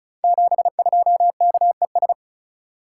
Morse code, heard as a keyed tone, about 36 wpm, 690 Hz. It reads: 72KES
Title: Code Groups - 5 Letter:Numbers - 35wpm